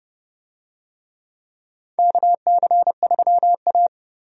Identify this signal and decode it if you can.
Morse code, heard as a keyed tone, about 30 wpm, 700 Hz. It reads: KC3A
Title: USA Call Signs - 4 Characters - 30wpm